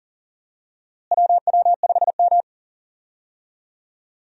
Morse code, heard as a keyed tone, about 40 wpm, 700 Hz. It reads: WW5M
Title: USA Call Signs - 4 Characters - Repeated 2x - 40wpm